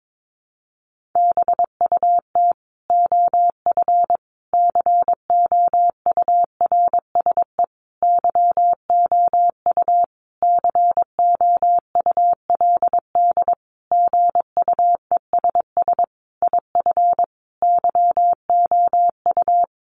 Morse code, heard as a keyed tone, about 22 wpm, 700 Hz. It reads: BUT OF COURSE YOU COULD GUESS IF YOU
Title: Sentences from Top 1500 Words 22wpm